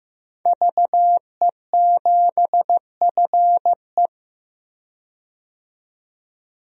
Morse code, heard as a keyed tone, about 16 wpm, 700 Hz. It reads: VE7FE
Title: DXCC-100 Call Signs - 5 Characters - 15wpm